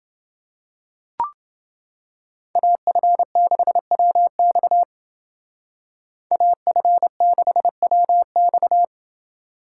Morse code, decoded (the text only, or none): AF6WX AF6WX